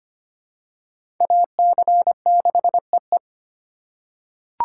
Morse code, heard as a keyed tone, about 25 wpm, 700 Hz. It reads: AC6EE
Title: USA Call Signs - 3x Character Spacing - 25wpm